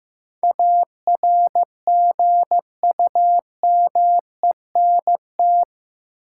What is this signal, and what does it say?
Morse code, heard as a keyed tone, about 15 wpm, 700 Hz. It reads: ARGUMENT
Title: Sentences from Top 1400 Words - New Word Drill - 15wpm